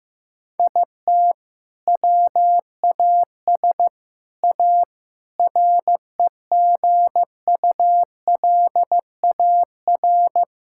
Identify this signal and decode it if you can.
Morse code, heard as a keyed tone, about 16 wpm, 700 Hz. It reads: IT WAS A REGULAR
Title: Sentences from Top 1000 Words - Review Part 6 - 15wpm